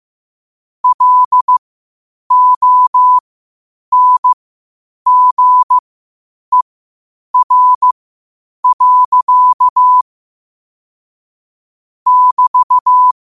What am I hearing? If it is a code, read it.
Morse code, heard as a keyed tone, about 15 wpm, spaced as for 8 wpm, 1000 Hz. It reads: LONGER. <BT>
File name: qotdfiles / 8wpm